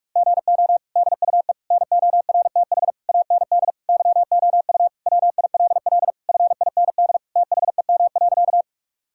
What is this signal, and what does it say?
Morse code, heard as a keyed tone, about 45 wpm, 700 Hz. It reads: GO DUE NORTH AND YOU WILL FIND THEM.
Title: Sentences from Top 1000 Words - Review Part 6 - 45wpm